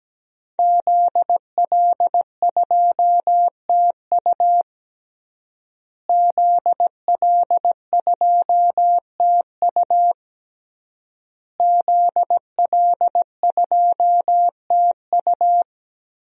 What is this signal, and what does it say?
Morse code, heard as a keyed tone, about 17 wpm, 700 Hz. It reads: ZL2TU ZL2TU ZL2TU
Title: DXCC-100 Call Signs - 5 Characters - Repeated 3x - 17wpm